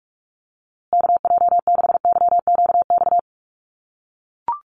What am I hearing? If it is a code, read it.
Morse code, heard as a keyed tone, about 44 wpm, 700 Hz. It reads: KJ6YQX
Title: USA Call Signs Repeated 3x 45wpm